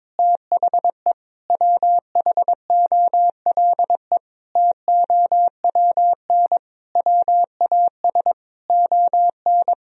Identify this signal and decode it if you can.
Morse code, heard as a keyed tone, about 22 wpm, 700 Hz. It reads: THE WHOLE TOWN WAS ON
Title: Sentences from Top 1000 Words - Review Part 2 - 22wpm